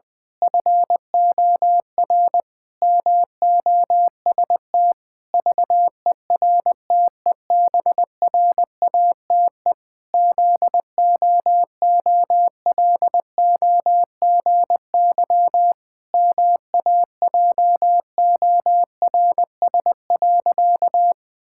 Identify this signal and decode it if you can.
Morse code, heard as a keyed tone, about 21 wpm, 700 Hz. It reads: FOR MOST VERTEBRATE ZOOLOGY MAJORS.
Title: Pangram Challenge v2 - 20wpm